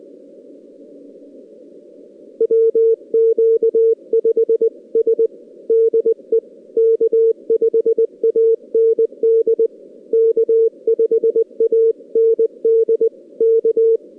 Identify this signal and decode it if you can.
Morse code, heard as a keyed tone, about 20 wpm, 450 Hz. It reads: WQ5S DE K5AND K5AND K
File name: contests / 2019-june=vhf-k5tr- -0101 0-2